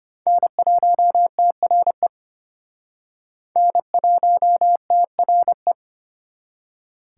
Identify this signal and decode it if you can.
Morse code, heard as a keyed tone, about 32 wpm, 700 Hz. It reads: N1TRE N1TRE